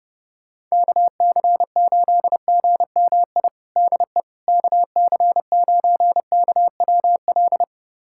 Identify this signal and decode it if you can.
Morse code, heard as a keyed tone, about 30 wpm, 700 Hz. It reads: KC8GMI DE KC9KWL
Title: USA Callsign DE Callsign - 6 Characters - 30wpm